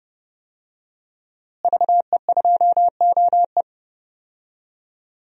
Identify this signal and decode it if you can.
Morse code, heard as a keyed tone, about 30 wpm, 700 Hz. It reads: VE2OE
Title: DXCC-100 Call Signs - 5 Characters - Repeated 2x - 30wpm